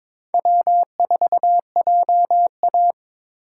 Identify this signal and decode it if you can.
Morse code, heard as a keyed tone, about 22 wpm, 700 Hz. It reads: W4JA
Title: USA Call Signs - 4 Characters - 8x Character Spacing - 22wpm